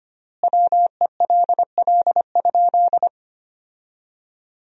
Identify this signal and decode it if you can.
Morse code, heard as a keyed tone, about 25 wpm, 700 Hz. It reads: WELL?
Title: 3x Word-Spacing - Sentences from Top 400 Words - 25wpm